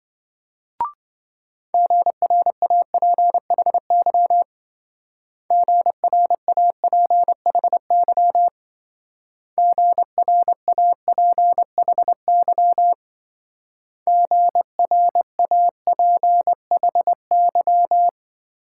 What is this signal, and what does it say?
Morse code, heard as a keyed tone, about 25 wpm, 700 Hz. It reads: GRAPHY GRAPHY GRAPHY GRAPHY